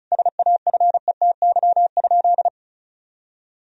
Morse code, heard as a keyed tone, about 34 wpm, 700 Hz. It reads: SAFETY?
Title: Sentences from Top 1000 Words - Review Part 4 - 35wpm